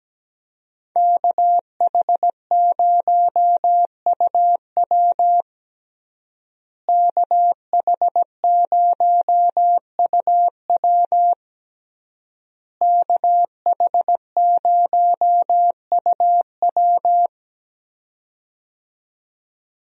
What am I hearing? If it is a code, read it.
Morse code, heard as a keyed tone, about 17 wpm, 700 Hz. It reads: KH0UW KH0UW KH0UW
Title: USA Call Signs - 5 Characters - Repeated 3x - 17wpm